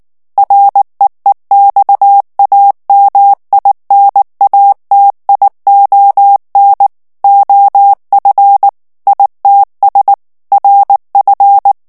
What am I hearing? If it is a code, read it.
Morse code, heard as a keyed tone, about 19 wpm, 800 Hz. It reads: REEXAMINATION OF ITS RF
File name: ANGLAIS texte clair 3 vitesse 18 mots minute